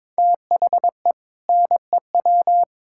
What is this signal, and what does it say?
Morse code, heard as a keyed tone, about 22 wpm, 700 Hz. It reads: THE NEW
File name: practice / 22wpm / Sentences with 4-Word Phrases - 22wpm